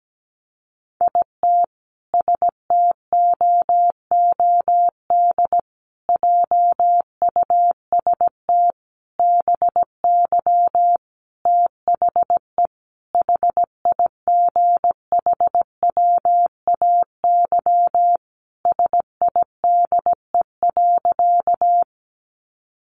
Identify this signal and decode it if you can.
Morse code, heard as a keyed tone, about 17 wpm, 700 Hz. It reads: IT STOOD JUST BY THE HIGHWAY SIDE.
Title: Sentences from Top 1400 Words 17wpm